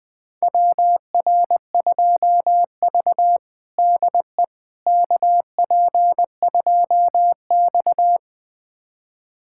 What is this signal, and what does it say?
Morse code, heard as a keyed tone, about 21 wpm, 700 Hz. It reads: WR2V DE KP2X
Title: USA Callsign DE Callsign - 4 Characters - 20wpm